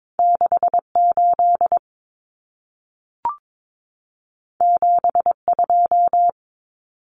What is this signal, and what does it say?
Morse code, heard as a keyed tone, about 22 wpm, 700 Hz. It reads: 68 72
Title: Common Cut Numbers - 2-Digit 22wpm